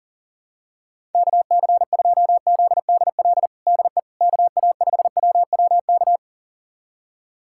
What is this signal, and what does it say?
Morse code, heard as a keyed tone, about 40 wpm, 700 Hz. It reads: KC2ZDL DE KA5WWK
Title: USA Callsign DE Callsign 40wpm